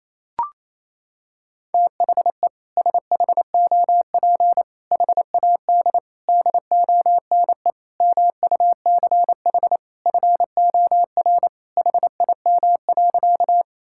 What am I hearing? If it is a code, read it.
Morse code, heard as a keyed tone, about 28 wpm, 700 Hz. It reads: THE SHOP HAD DONE MUCH FOR HIM.
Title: Sentences from Top 1600 Words 28wpm